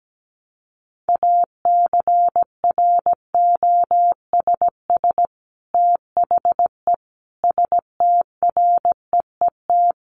Morse code, heard as a keyed tone, about 17 wpm, 700 Hz. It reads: ACROSS THE STREET
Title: Sets of 5 Words Encore 17wpm